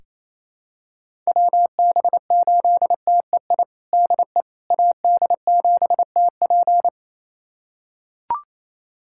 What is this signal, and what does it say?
Morse code, heard as a keyed tone, about 28 wpm, 700 Hz. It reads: WB8TEI DE AD7TP
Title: USA Callsign DE Callsign 28wpm